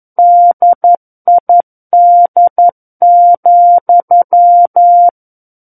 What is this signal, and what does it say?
Morse code, heard as a keyed tone, about 11 wpm, 700 Hz. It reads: DID,
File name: 11 WPM